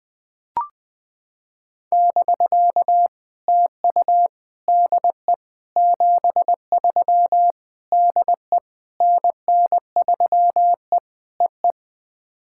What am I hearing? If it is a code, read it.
Morse code, heard as a keyed tone, about 20 wpm, 690 Hz. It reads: <BK> TU DE 73 DE NN3E EE
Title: POTA QSO Practice - 20wpm